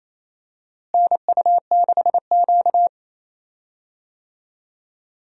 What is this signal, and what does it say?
Morse code, heard as a keyed tone, about 28 wpm, 700 Hz. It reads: NU6Q
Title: USA Call Signs - 4 Characters - Repeated 3x - 28wpm